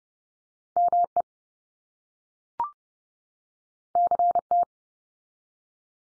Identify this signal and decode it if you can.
Morse code, heard as a keyed tone, about 30 wpm, 700 Hz. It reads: ME CT
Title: US States 30WPM